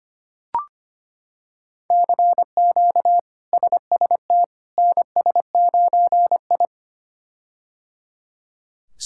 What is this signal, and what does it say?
Morse code, heard as a keyed tone, about 25 wpm, 700 Hz. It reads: CQ SST NS9I
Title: K1USN SST QSO Practice - 25wpm